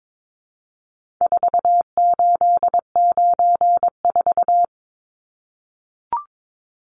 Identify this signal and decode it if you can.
Morse code, heard as a keyed tone, about 22 wpm, 700 Hz. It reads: 4894
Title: Numbers - 4-Digit 22wpm